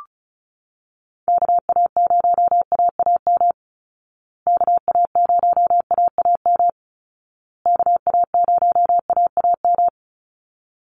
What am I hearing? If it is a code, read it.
Morse code, heard as a keyed tone, about 34 wpm, 700 Hz. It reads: KA0AAM KA0AAM KA0AAM
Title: Call Signs Repeated 2x - 35wpm